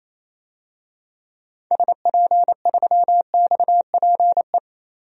Morse code, heard as a keyed tone, about 28 wpm, 700 Hz. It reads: SP3XPE
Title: DXCC-100 Call Signs - 6 Characters - Repeated 2x - 28wpm